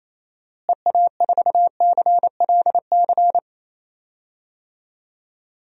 Morse code, heard as a keyed tone, about 27 wpm, 700 Hz. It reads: EA4CLC
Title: DXCC-100 Call Signs - 6 Characters - Repeated 2x - 28wpm